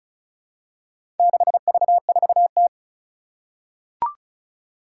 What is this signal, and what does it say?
Morse code, heard as a keyed tone, about 34 wpm, 700 Hz. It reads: 6V4T
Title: Four Challenging Letter-Number Combinations 35wpm